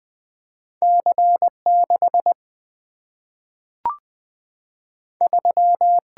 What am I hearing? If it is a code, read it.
Morse code, heard as a keyed tone, about 20 wpm, 700 Hz. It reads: C 6 3
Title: Single-Letter Single-Number 20wpm